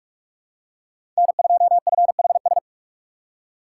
Morse code, heard as a keyed tone, about 46 wpm, 700 Hz. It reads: N2FHS
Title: USA Call Signs - 3x Character Spacing - 45wpm